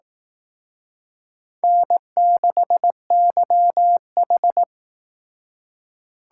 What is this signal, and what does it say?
Morse code, heard as a keyed tone, about 18 wpm, 700 Hz. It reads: N6YH
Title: Speed Racing - USA 1x2 Call Signs 15wpm